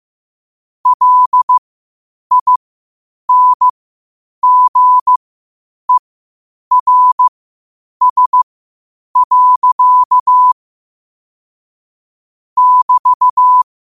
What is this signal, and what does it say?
Morse code, heard as a keyed tone, about 15 wpm, spaced as for 8 wpm, 1000 Hz. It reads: LINGERS. <BT>
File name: qotdfiles / 8wpm